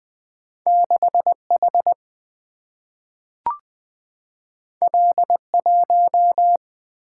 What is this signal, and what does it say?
Morse code, heard as a keyed tone, about 20 wpm, 700 Hz. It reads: 6 H L 1
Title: Single-Letter Single-Number 20wpm